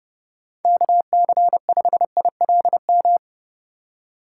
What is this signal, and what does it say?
Morse code, heard as a keyed tone, about 34 wpm, 700 Hz. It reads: KC5ILM